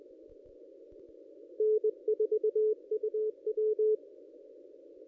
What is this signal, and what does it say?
Morse code, heard as a keyed tone, about 20 wpm, 430 Hz. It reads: N4UW
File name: contests / ss-cw-k5tr-op-n5rz -1400